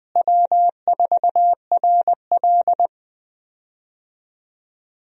Morse code, heard as a keyed tone, about 20 wpm, 700 Hz. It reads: W4RL
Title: USA Call Signs - 4 Characters - Repeated 2x - 20wpm